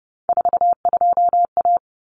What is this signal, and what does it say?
Morse code, heard as a keyed tone, about 30 wpm, 700 Hz. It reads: 42A
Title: Common Cut Numbers - 3-Digit 30wpm